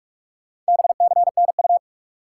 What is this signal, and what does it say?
Morse code, heard as a keyed tone, about 46 wpm, 700 Hz. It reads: BCNU
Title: Top 90 QSO Elements 45wpm